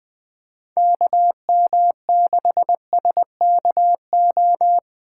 Morse code, heard as a keyed tone, about 20 wpm, 700 Hz. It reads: KM6SKO
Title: USA Call Signs - 6 Characters - 8x Character Spacing - 20wpm